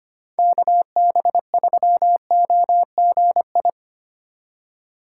Morse code, heard as a keyed tone, about 25 wpm, 700 Hz. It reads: KB3OGI
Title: USA Call Signs - 4x Character Spacing - 25wpm